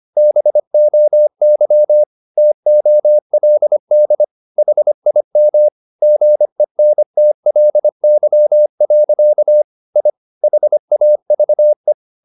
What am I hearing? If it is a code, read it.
Morse code, heard as a keyed tone, about 25 wpm, 600 Hz. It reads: BOY TOLD HIM GENTLY. I HAVE
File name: the old man and the sea part 1 25wpm